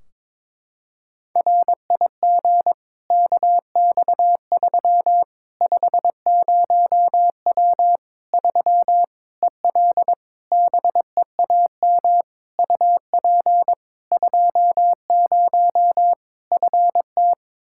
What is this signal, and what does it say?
Morse code, heard as a keyed tone, about 22 wpm, 700 Hz. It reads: RIG KX3 50W 3 EL BEAM UP 20 FT
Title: Practice QSO - Exchange 2 - Info - 22wpm